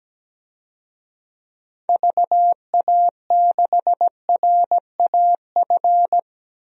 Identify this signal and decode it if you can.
Morse code, heard as a keyed tone, about 17 wpm, 700 Hz. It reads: VA6RAF
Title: DXCC-100 Call Signs - 6 Characters - Repeated 2x - 17wpm